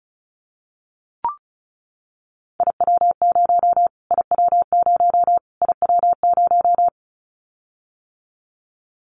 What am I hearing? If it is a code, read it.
Morse code, heard as a keyed tone, about 34 wpm, 700 Hz. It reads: IW0 IW0 IW0
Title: Callsign Prefixes for Top 25 DXCC - Repeated 3x - 35wpm